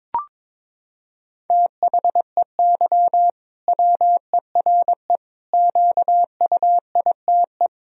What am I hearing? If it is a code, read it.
Morse code, heard as a keyed tone, about 22 wpm, 700 Hz. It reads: THEY WERE QUITE
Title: Sentences from Top 1000 Words - Review Part 3 - 22wpm